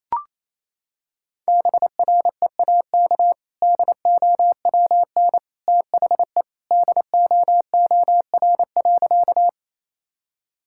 Morse code, heard as a keyed tone, about 28 wpm, 700 Hz. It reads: BREAK DOWN THE DOOR.
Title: Sentences from Top 1000 Words - Review Part 6 - 28wpm